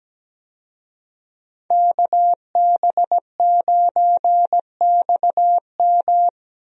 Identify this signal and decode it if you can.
Morse code, heard as a keyed tone, about 17 wpm, 700 Hz. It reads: KB9XM
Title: USA Call Signs - 5 Characters - Repeated 3x - 17wpm